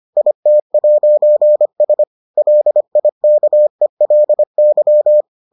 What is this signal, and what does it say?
Morse code, heard as a keyed tone, about 25 wpm, 600 Hz. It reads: IT'S LIKELY